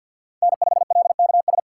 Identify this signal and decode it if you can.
Morse code, heard as a keyed tone, about 52 wpm, 700 Hz. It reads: N5LKS
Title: USA Call Signs - 5 Characters - 6x Character Spacing - 50wpm